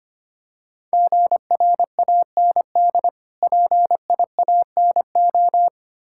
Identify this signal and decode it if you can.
Morse code, heard as a keyed tone, about 26 wpm, 700 Hz. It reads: GRAND PIANO
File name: practice / 25wpm / Difficult Words - Speed-Racing with Hints - 25wpm